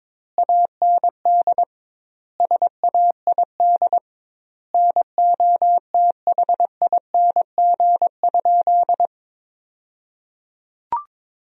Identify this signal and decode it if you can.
Morse code, heard as a keyed tone, about 21 wpm, 700 Hz. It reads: AND SAID NOTHING?
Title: 2x Word-Spacing - Sentences from Top 500 Words - 22wpm